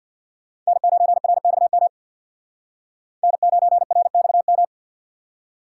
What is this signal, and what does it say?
Morse code, heard as a keyed tone, about 62 wpm, 700 Hz. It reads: N9RXM N9RXM
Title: Speed Racing - USA 1x3 Call Signs 50wpm